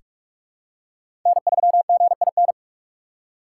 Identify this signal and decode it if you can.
Morse code, heard as a keyed tone, about 46 wpm, 700 Hz. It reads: N3GIN
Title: USA Call Signs - 5 Characters - 6x Character Spacing - 45wpm